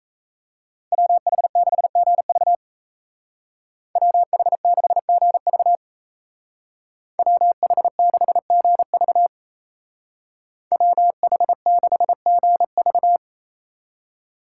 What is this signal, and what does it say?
Morse code, heard as a keyed tone, about 36 wpm, 700 Hz. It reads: WH6GV WH6GV WH6GV WH6GV